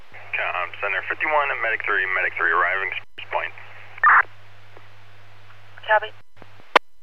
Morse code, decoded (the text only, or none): none